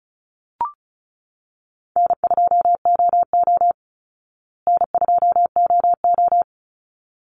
Morse code, heard as a keyed tone, about 34 wpm, 700 Hz. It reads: N2OO N2OO
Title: Easy Call Signs Repeated 2x - 35wpm